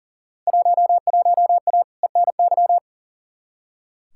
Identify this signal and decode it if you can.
Morse code, heard as a keyed tone, about 39 wpm, 700 Hz. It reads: 11A ENY
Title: Field Day Exchanges Encore - 40wpm